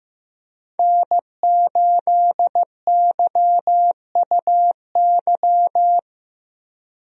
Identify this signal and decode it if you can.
Morse code, heard as a keyed tone, about 15 wpm, 700 Hz. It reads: N8YUY
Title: USA Call Signs - 5 Characters - 15wpm